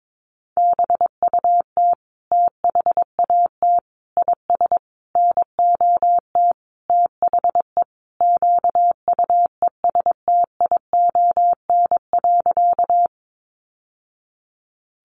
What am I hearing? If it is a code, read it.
Morse code, heard as a keyed tone, about 22 wpm, 700 Hz. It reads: BUT THAT IS NOT THE QUESTION.
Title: Sentences from Top 600 Words 22wpm